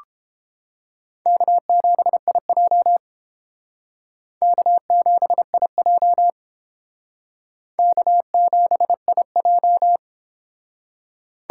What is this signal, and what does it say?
Morse code, heard as a keyed tone, about 29 wpm, 700 Hz. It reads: K7IJ K7IJ K7IJ